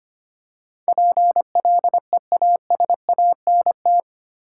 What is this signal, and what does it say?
Morse code, heard as a keyed tone, about 25 wpm, 700 Hz. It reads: PLEASANT
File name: 8 Letter Words - First 3 Letters Given - 25wpm